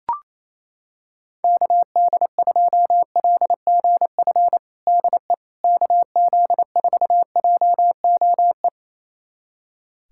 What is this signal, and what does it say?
Morse code, heard as a keyed tone, about 28 wpm, 700 Hz. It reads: KD2LGF DE KZ4JOE
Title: USA Callsign DE Callsign - 6 Characters - 28wpm